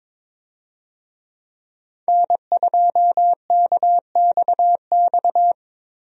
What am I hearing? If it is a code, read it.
Morse code, heard as a keyed tone, about 22 wpm, 700 Hz. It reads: N2KXX
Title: USA Call Signs - 5 Characters - Repeated 3x - 22wpm